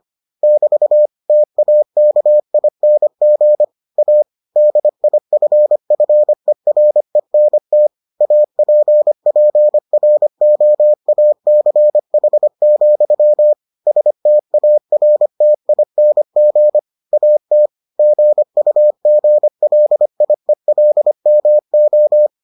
Morse code, heard as a keyed tone, about 25 wpm, 600 Hz. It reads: <BT> TAKING A DIFFERENT APPROACH, STARTING AT GUGLIELMO